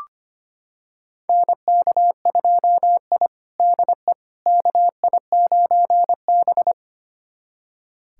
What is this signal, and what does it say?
Morse code, heard as a keyed tone, about 25 wpm, 700 Hz. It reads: NK2I DE KI9B
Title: USA Callsign DE Callsign - 4 Characters - 25wpm